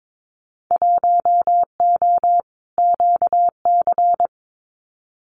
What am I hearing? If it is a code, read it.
Morse code, heard as a keyed tone, about 22 wpm, 700 Hz. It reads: 1O QC
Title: Winter Field Day Exchanges 22wpm